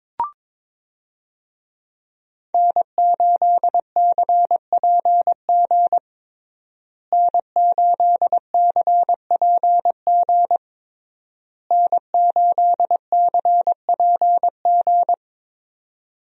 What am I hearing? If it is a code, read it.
Morse code, heard as a keyed tone, about 22 wpm, 700 Hz. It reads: N8CPG N8CPG N8CPG
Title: USA Call Signs - 5 Characters - Repeated 3x - 22wpm